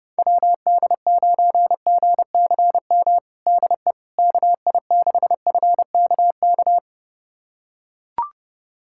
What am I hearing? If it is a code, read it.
Morse code, heard as a keyed tone, about 30 wpm, 700 Hz. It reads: WD9GCM DE KI6FKK
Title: USA Callsign DE Callsign - 6 Characters - 30wpm